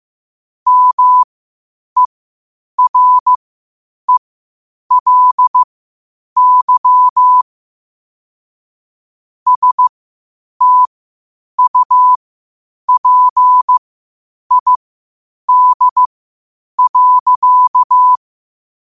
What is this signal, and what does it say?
Morse code, heard as a keyed tone, about 15 wpm, spaced as for 8 wpm, 1000 Hz. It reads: MERELY STUPID.
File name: qotdfiles / 8wpm